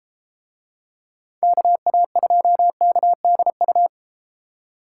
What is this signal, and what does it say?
Morse code, heard as a keyed tone, about 30 wpm, 700 Hz. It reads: KA2KDU